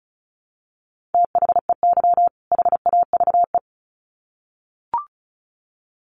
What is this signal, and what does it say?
Morse code, heard as a keyed tone, about 34 wpm, 700 Hz. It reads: THEY HAVE
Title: Sets of 2 Words 35wpm